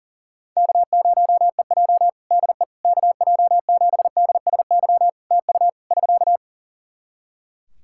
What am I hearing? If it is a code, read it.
Morse code, heard as a keyed tone, about 40 wpm, 700 Hz. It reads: K0EJ DE KJ7DSY TU <SK>
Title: Practice QSO - Exchange 4 - Ending - 40wpm